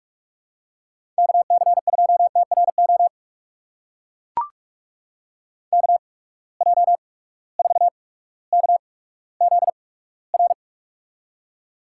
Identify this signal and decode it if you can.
Morse code, heard as a keyed tone, about 44 wpm, spaced as for 19 wpm, 700 Hz. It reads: KC2TRO KJ4KZR
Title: USA Call Signs - 6 Characters - 8x Character Spacing - 45wpm